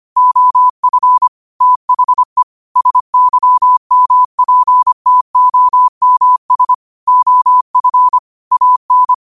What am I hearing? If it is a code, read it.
Morse code, heard as a keyed tone, about 25 wpm, 1000 Hz. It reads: OF THE SYMPTOMS OF AN
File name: qotdfiles / 25wpm